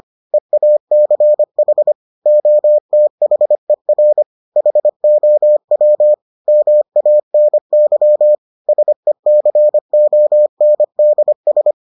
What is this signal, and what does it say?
Morse code, heard as a keyed tone, about 25 wpm, 600 Hz. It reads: EACH OTHER HOW MANY SECONDS